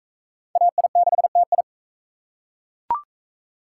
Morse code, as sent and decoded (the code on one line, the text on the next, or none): .- .. -.... - ..
AI6TI